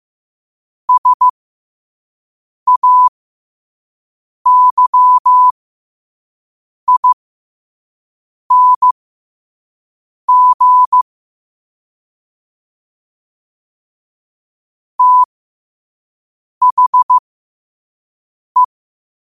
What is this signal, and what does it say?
Morse code, heard as a keyed tone, about 15 wpm, spaced as for 5 wpm, 1000 Hz. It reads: SAYING THE
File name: qotdfiles / 5wpm